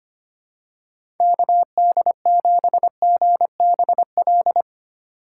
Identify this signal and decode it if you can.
Morse code, heard as a keyed tone, about 25 wpm, 700 Hz. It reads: KD7GBL
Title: USA Call Signs - 6 Characters - 3x Character Spacing - 25wpm